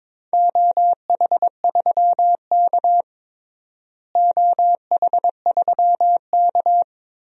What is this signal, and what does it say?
Morse code, heard as a keyed tone, about 22 wpm, 700 Hz. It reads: OH3K OH3K
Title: DXCC-100 Call Signs - 4 Characters - Repeated 3x - 22wpm